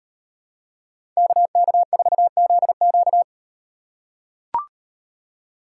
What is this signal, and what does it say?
Morse code, heard as a keyed tone, about 37 wpm, 700 Hz. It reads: KK4ZQ